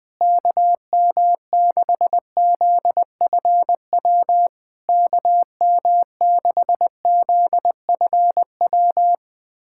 Morse code, heard as a keyed tone, about 20 wpm, 690 Hz. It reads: KM6ZFW KM6ZFW
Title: SOTA QSO Practice 20wpm